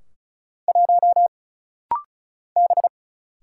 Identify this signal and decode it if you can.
Morse code, heard as a keyed tone, about 36 wpm, 700 Hz. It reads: 1 B
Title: Single Letter-Number Alt - Rapid-Fire 35wpm